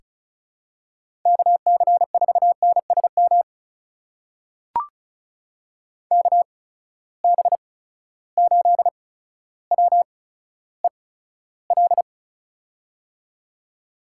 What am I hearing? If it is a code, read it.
Morse code, heard as a keyed tone, about 36 wpm, spaced as for 15 wpm, 700 Hz. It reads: KC4NSM KB8WEL
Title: USA Call Signs - 6 Characters - 8x Character Spacing - 35wpm